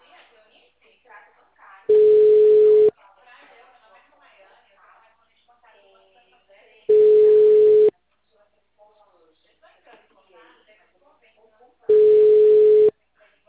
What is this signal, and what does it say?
Ringback tone (Europe, 425 Hz), with speech.